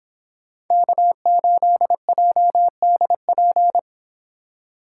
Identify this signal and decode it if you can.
Morse code, heard as a keyed tone, about 26 wpm, 700 Hz. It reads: K8JDP